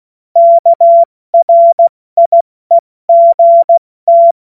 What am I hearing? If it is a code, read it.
Morse code, heard as a keyed tone, about 16 wpm, 680 Hz. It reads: KRIEGT
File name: Morsekurs / Witze / Witz 29. Juni 2021, 70 F80